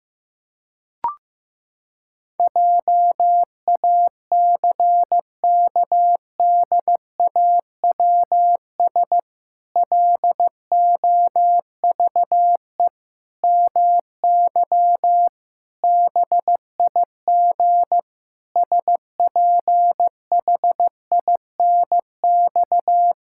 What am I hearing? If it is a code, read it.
Morse code, heard as a keyed tone, about 15 wpm, 700 Hz. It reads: JACKDAWS LOVE MY BIG SPHINX
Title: Pangram Challenge v2 - 15wpm